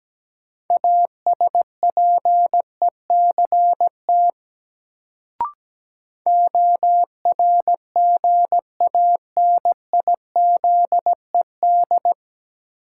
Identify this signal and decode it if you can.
Morse code, heard as a keyed tone, about 17 wpm, 700 Hz. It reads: ASPECT ORGANIZED
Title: Sentences from Top 1300 Words - New Word Drill Alt - 17wpm